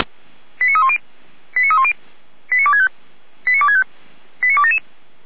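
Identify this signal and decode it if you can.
Five-tone selective call (ZVEI), heard as quick runs of five tones, tones of about 70 ms. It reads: 89310 89310 89265 89265 89277
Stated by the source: Five-tone ZVEI selective call received by SDR, from audio.f85360.feuerwehr-orlamuende.org fire brigade server.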